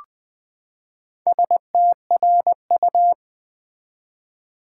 Morse code, heard as a keyed tone, about 20 wpm, 700 Hz. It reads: STRU
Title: Build A Word - 4 Letters At A Time - 8-Letter Words - 20wpm